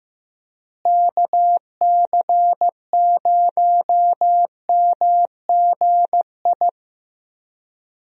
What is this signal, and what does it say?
Morse code, heard as a keyed tone, about 15 wpm, 700 Hz. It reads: KC0MGI
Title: USA Call Signs - 6 Characters - Repeated 3x - 15wpm